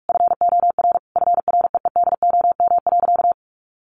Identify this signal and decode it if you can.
Morse code, heard as a keyed tone, about 44 wpm, 700 Hz. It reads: FOR FREEDOM.
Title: Sentences from Top 1100 Words 45wpm